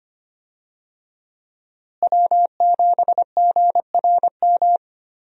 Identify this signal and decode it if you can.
Morse code, heard as a keyed tone, about 25 wpm, 700 Hz. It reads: W7GRM
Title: USA Call Signs - 5 Characters - Repeated 2x - 25wpm